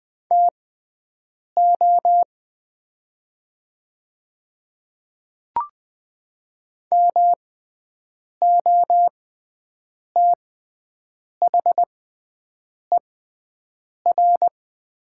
Morse code, heard as a keyed tone, about 20 wpm, spaced as for 7 wpm, 700 Hz. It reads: TO MOTHER
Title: 6x Character Spacing - Top 200 Words - 20wpm